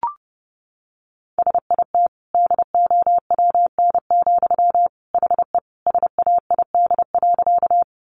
Morse code, heard as a keyed tone, about 30 wpm, 700 Hz. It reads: SIT DOWN, HE SAID.
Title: Sentences from Top 500 Words 30wpm